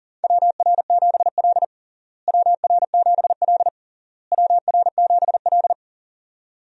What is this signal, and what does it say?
Morse code, heard as a keyed tone, about 40 wpm, 700 Hz. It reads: WR7L WR7L WR7L
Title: USA Call Signs - 4 Characters - Repeated 3x - 40wpm